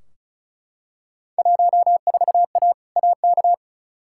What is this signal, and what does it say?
Morse code, heard as a keyed tone, about 36 wpm, 700 Hz. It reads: 14A AK
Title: Field Day Exchanges Encore - 35wpm